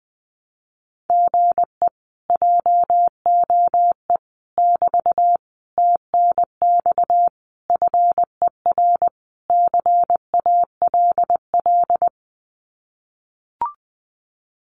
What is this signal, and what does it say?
Morse code, heard as a keyed tone, about 21 wpm, 700 Hz. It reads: GE JOE <BT> TNX FER CALL
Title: Common QSO Phrases Encore 20wpm